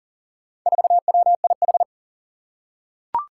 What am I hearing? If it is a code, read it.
Morse code, heard as a keyed tone, about 42 wpm, 700 Hz. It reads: 4WIH
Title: Four Challenging Letter-Number Combinations 40wpm